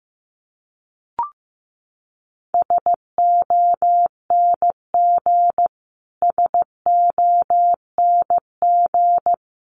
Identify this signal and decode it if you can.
Morse code, heard as a keyed tone, about 15 wpm, 700 Hz. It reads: SONG SONG
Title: Top 500 Words Repeated 15wpm